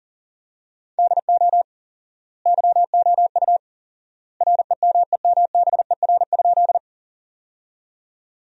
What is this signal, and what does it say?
Morse code, heard as a keyed tone, about 40 wpm, 700 Hz. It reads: DO YOU REMEMBER?
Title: 4x Word-Spacing - Sentences from Top 400 Words - 40wpm